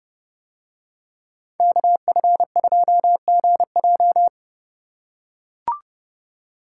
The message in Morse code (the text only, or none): KF2GJ